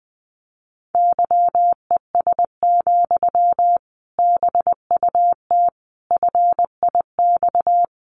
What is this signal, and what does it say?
Morse code, heard as a keyed tone, about 20 wpm, 700 Hz. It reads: YES, BUT FIX
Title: Sentences from Top 1400 Words 20wpm